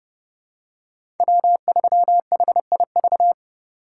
Morse code, heard as a keyed tone, about 30 wpm, 700 Hz. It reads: W3HIV
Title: USA Call Signs - 5 Characters - 3x Character Spacing - 30wpm